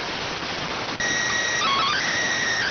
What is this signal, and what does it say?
SSTV (slow-scan television), PD 290.